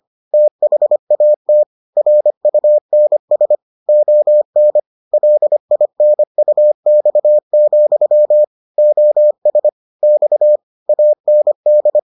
Morse code, heard as a keyed tone, about 25 wpm, 600 Hz. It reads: THAT RUNS ON LINUX, OS X AND